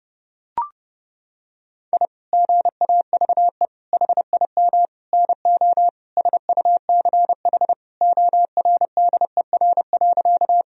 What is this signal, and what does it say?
Morse code, heard as a keyed tone, about 32 wpm, 700 Hz. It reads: I GAVE HIM NO SUCH ORDER.
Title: Sentences from Top 1000 Words - Review Part 2 - 30wpm